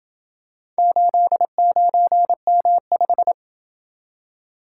Morse code, heard as a keyed tone, about 28 wpm, 700 Hz. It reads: 89M5